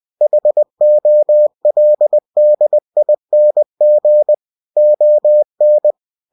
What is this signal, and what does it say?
Morse code, heard as a keyed tone, about 20 wpm, 600 Hz. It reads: HOLDING ON
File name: cw-books / winnie-the-pooh / winnie-20wpm-page-0006